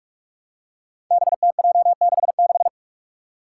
Morse code, heard as a keyed tone, about 41 wpm, 700 Hz. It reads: BTJ66
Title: Five Challenging Letter-Number Combinations 45wpm